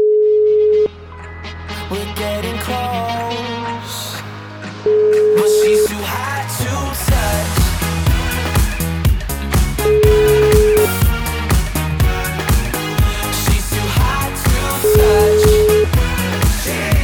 Ringback tone (Europe, 425 Hz), with noise.